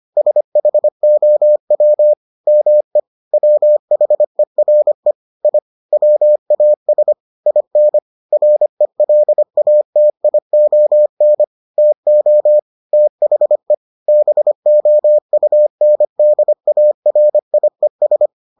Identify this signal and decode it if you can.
Morse code, heard as a keyed tone, about 25 wpm, 600 Hz. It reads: SHOW ME WHERE I WAS IN RELATION TO THE BOUNDARIES